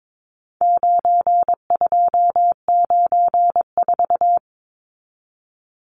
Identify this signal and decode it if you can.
Morse code, heard as a keyed tone, about 22 wpm, 700 Hz. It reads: 9294
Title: Numbers - 4-Digit 22wpm